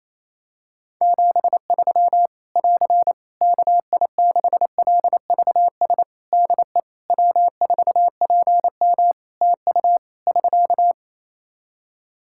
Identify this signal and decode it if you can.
Morse code, heard as a keyed tone, about 27 wpm, 700 Hz. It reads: 73 <AR> KI6LVS DE W4PM TU <SK>
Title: Practice QSO - Exchange 4 - Ending - 28wpm